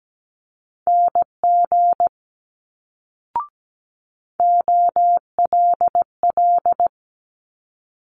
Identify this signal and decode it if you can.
Morse code, heard as a keyed tone, about 17 wpm, 700 Hz. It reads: NG OLL
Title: Multi-letter Phonemes Quick Fire 17wpm